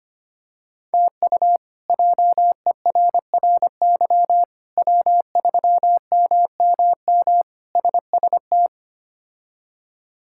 Morse code, heard as a keyed tone, about 25 wpm, 700 Hz. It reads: TU JERRY W3MMM SST
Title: K1USN SST QSO Practice - 25wpm